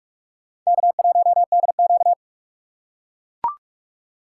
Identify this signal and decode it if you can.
Morse code, heard as a keyed tone, about 43 wpm, 700 Hz. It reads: K1DQ
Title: USA Call Signs - 4 Characters - 4x Character Spacing - 45wpm